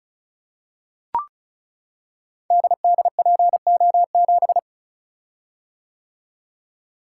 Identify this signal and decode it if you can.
Morse code, heard as a keyed tone, about 36 wpm, 700 Hz. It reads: DDPO7
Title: Code Groups - 5 Letter:Numbers - 35wpm